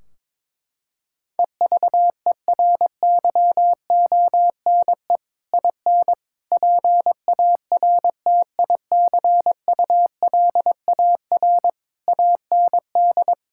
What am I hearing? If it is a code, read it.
Morse code, heard as a keyed tone, about 21 wpm, 700 Hz. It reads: EVERYONE IN PARTICULAR AND
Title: Sentences from Top 1600 Words 22wpm